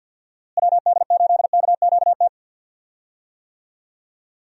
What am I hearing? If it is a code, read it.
Morse code, heard as a keyed tone, about 47 wpm, 700 Hz. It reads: WD8KQT
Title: USA Call Signs - 6 Characters - Repeated 3x - 50wpm